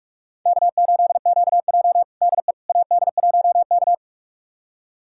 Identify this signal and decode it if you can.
Morse code, heard as a keyed tone, about 45 wpm, 700 Hz. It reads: K8QJ DE AD1K
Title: USA Callsign DE Callsign - 4 Characters - 45wpm